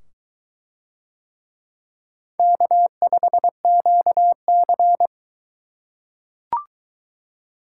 Morse code, heard as a keyed tone, about 22 wpm, 700 Hz. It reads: K5QC